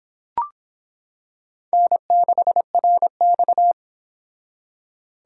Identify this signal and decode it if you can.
Morse code, heard as a keyed tone, about 25 wpm, 700 Hz. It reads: N6RX